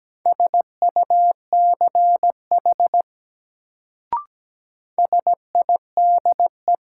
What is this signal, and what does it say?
Morse code, heard as a keyed tone, about 17 wpm, 700 Hz. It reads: SUCH SIDE
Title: Sets of 2 Words - New Word Drill - 17wpm